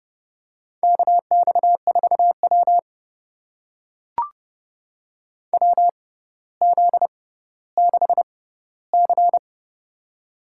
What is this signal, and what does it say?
Morse code, heard as a keyed tone, about 31 wpm, spaced as for 16 wpm, 700 Hz. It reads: KX4W WZ6C
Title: USA Call Signs - 4 Characters - 6x Character Spacing - 30wpm